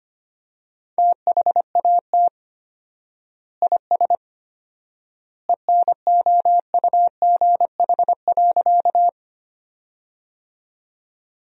Morse code, heard as a keyed tone, about 25 wpm, 700 Hz. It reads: THAT IS ENOUGH.
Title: 4x Word-Spacing - Sentences from Top 400 Words - 25wpm